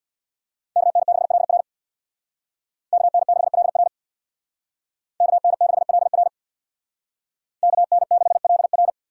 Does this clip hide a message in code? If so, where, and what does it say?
Yes: Morse code from 0.8 to 8.9 s, heard as a keyed tone, about 64 wpm, 700 Hz. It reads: KN6LR KN6LR KN6LR KN6LR